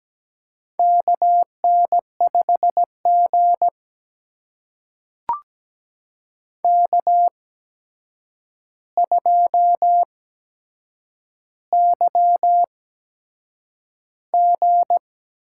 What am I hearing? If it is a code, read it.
Morse code, heard as a keyed tone, about 17 wpm, spaced as for 7 wpm, 700 Hz. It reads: KN5G K2YG
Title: USA Call Signs - 4 Characters - 8x Character Spacing - 17wpm